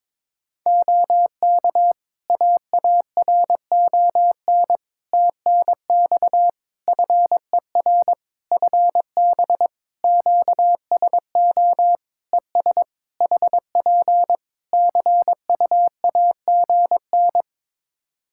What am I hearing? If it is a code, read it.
Morse code, heard as a keyed tone, about 22 wpm, 700 Hz. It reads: OK AARON TNX FER FB QSO ES HP CUAGN
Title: K1USN SST QSO Practice 22wpm